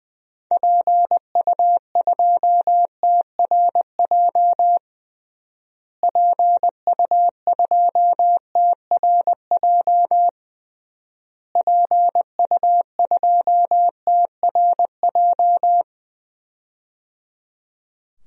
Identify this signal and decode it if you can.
Morse code, heard as a keyed tone, about 20 wpm, 700 Hz. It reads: PU2TRJ PU2TRJ PU2TRJ
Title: DXCC-100 Call Signs - 6 Characters - Repeated 3x - 20wpm